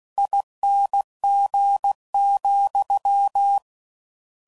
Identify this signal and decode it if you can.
Morse code, heard as a keyed tone, about 16 wpm, 800 Hz. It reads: ING,
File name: Suffixes / General / ESE INGs